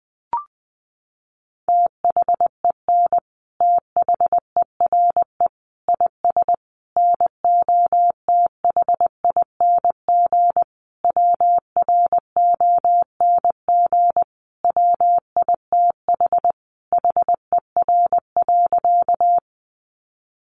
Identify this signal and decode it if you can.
Morse code, heard as a keyed tone, about 20 wpm, 700 Hz. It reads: THEN THERE IS NOTHING WRONG WITH HER.
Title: Sentences from Top 900 Words 20wpm